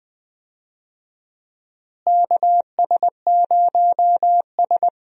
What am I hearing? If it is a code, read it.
Morse code, heard as a keyed tone, about 20 wpm, 700 Hz. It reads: KS0S
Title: USA Call Signs - 4 Characters - Repeated 2x - 20wpm